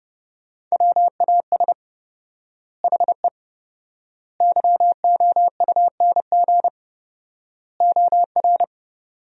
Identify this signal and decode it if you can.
Morse code, heard as a keyed tone, about 29 wpm, 700 Hz. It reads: WAS HE YOUNG OR
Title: 4x Word-Spacing - Sentences from Top 400 Words - 30wpm